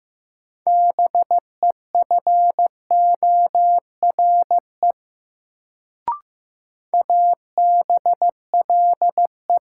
Morse code, heard as a keyed tone, about 15 wpm, 700 Hz. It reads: BEFORE ABLE
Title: Sets of 2 Words - New Word Drill - 15wpm